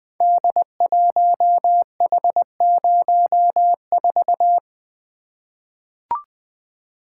Morse code, heard as a keyed tone, about 20 wpm, 700 Hz. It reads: D1H04
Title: Code Groups - 5 Letter:Numbers - 20wpm